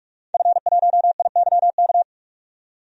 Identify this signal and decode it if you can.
Morse code, heard as a keyed tone, about 48 wpm, 700 Hz. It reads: F1IYK
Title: DXCC-100 Call Signs - 5 Characters - 45wpm